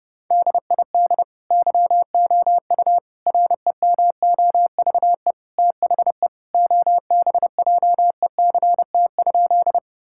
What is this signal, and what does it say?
Morse code, heard as a keyed tone, about 30 wpm, 700 Hz. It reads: DID YOU REMOVE THE OBJECT?
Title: Sentences from Top 1000 Words - Review Part 5 - 30wpm